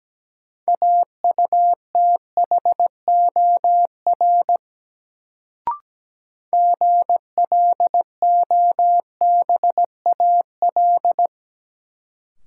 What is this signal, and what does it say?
Morse code, heard as a keyed tone, about 17 wpm, 700 Hz. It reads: AUTHOR GLOBAL
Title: Sentences from Top 1200 Words - New Word Drill - 17wpm